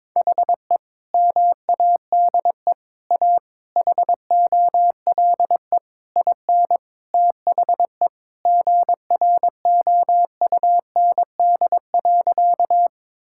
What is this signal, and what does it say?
Morse code, heard as a keyed tone, about 22 wpm, 700 Hz. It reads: HE MADE A HOLE IN THE GROUND.
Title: Sentences from Top 1600 Words 22wpm